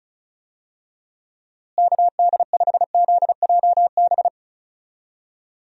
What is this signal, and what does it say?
Morse code, heard as a keyed tone, about 36 wpm, 700 Hz. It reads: KD5ZJB
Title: USA Call Signs - 6 Characters - Repeated 3x - 35wpm